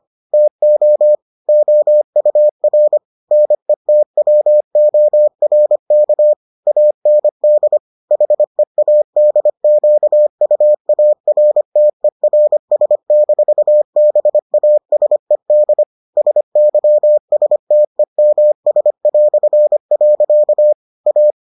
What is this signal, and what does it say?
Morse code, heard as a keyed tone, about 25 wpm, 600 Hz. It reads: TO OUR NETWORK AND HEADQUARTERS-BASED SYSTEMS". A